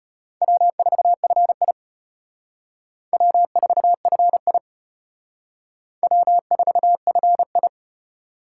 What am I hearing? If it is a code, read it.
Morse code, heard as a keyed tone, about 31 wpm, 700 Hz. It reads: W4FI W4FI W4FI
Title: Speed Racing - USA 1x2 Call Signs 25wpm